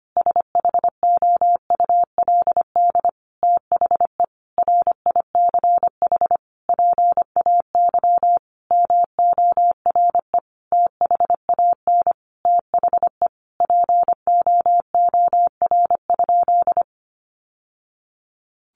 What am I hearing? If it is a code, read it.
Morse code, heard as a keyed tone, about 25 wpm, 700 Hz. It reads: SHOULD THE RICH PAY MORE THAN THE POOR?
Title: Sentences from Top 1100 Words 25wpm